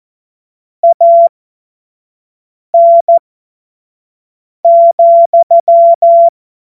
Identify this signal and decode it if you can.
Morse code, heard as a keyed tone, about 14 wpm, spaced as for 5 wpm, 680 Hz. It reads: AN,
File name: Morsekurs / Witze / Witz 21. Februar 2023 25 F70